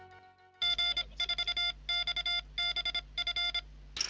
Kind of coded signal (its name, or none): Morse code